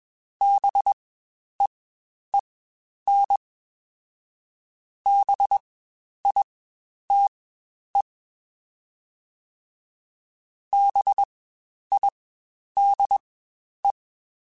Morse code, heard as a keyed tone, about 21 wpm, spaced as for 10 wpm, 800 Hz. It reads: BEEN BITE BIDE
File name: Spaced Alphas / Spaced Alphas mp3s / Bs sp @20